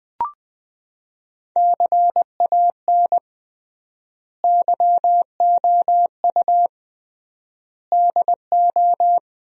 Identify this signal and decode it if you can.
Morse code, heard as a keyed tone, about 20 wpm, 700 Hz. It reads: CAN YOU DO
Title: 3x Word-Spacing - Sentences from Top 100 Words - 20wpm